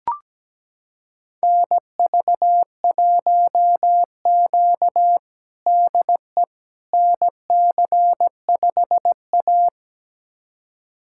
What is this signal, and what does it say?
Morse code, heard as a keyed tone, about 17 wpm, 700 Hz. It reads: NV1Q DE NC5A
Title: USA Callsign DE Callsign - 4 Characters - 17wpm